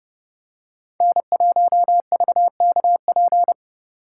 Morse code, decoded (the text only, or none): N1VKP